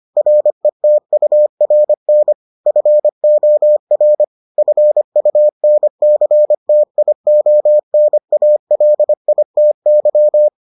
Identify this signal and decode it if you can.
Morse code, heard as a keyed tone, about 25 wpm, 600 Hz. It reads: RETURN FOR FUNCTIONALITY